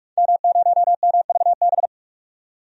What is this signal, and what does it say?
Morse code, heard as a keyed tone, about 41 wpm, 700 Hz. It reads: M0MVB
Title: DXCC-100 Call Signs - 5 Characters - 45wpm